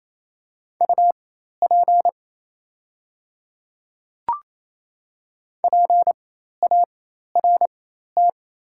Morse code, heard as a keyed tone, about 27 wpm, spaced as for 13 wpm, 700 Hz. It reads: UP PART
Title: 4x Character Spacing - Top 200 Words - 28wpm